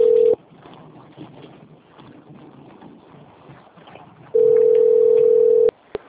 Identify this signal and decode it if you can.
Ringback tone (North America), 440 + 480 Hz.
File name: out- - - - - .11198